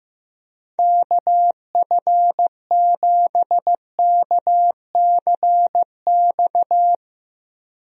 Morse code, heard as a keyed tone, about 16 wpm, 700 Hz. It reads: KF7KCX
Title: USA Call Signs - 6 Characters - Repeated 3x - 15wpm